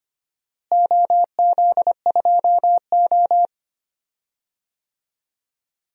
Morse code, heard as a keyed tone, about 25 wpm, 700 Hz. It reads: OZ2O
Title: DXCC-100 Call Signs - 4 Characters - Repeated 2x - 25wpm